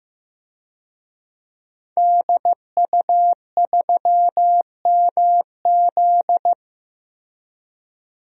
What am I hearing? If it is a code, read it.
Morse code, heard as a keyed tone, about 15 wpm, 700 Hz. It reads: DU3MZ
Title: DXCC-100 Call Signs - 5 Characters - Repeated 2x - 15wpm